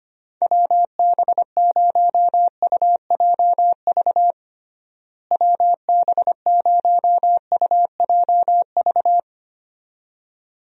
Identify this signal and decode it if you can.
Morse code, heard as a keyed tone, about 25 wpm, 700 Hz. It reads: WB0UJV WB0UJV
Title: USA Call Signs - 6 Characters - Repeated 2x - 25wpm